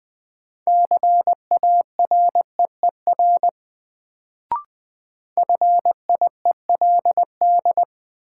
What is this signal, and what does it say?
Morse code, heard as a keyed tone, about 20 wpm, 700 Hz. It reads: CAREER FIELD
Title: Sentences from Top 1000 Words - New Word Drill - 20wpm